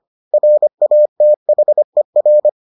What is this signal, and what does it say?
Morse code, heard as a keyed tone, about 25 wpm, 600 Hz. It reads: RATHER